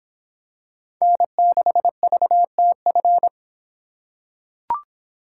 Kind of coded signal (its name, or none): Morse code